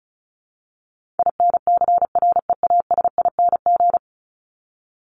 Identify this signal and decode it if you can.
Morse code, heard as a keyed tone, about 34 wpm, 700 Hz. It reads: INCREASING
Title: ING Suffix - Top 100 Words with Base Words 35wpm